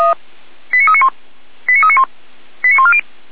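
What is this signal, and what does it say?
Five-tone selective call (ZVEI), heard as quick runs of five tones, tones of about 70 ms. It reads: 89391 89391 89140
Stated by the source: ZVEI five-tone selective call received by SDR.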